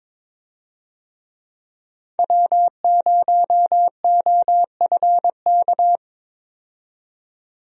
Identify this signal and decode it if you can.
Morse code, heard as a keyed tone, about 22 wpm, 700 Hz. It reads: W0OFK
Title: USA Call Signs - 5 Characters - 22wpm